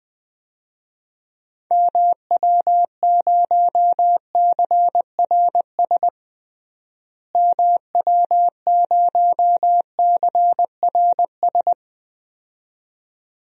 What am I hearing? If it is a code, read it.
Morse code, heard as a keyed tone, about 20 wpm, 700 Hz. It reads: MW0CRS MW0CRS
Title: DXCC-100 Call Signs - 6 Characters - Repeated 2x - 20wpm